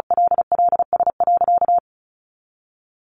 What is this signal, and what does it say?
Morse code, heard as a keyed tone, about 34 wpm, 700 Hz. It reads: LLS.
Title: Easy Phrases 35wpm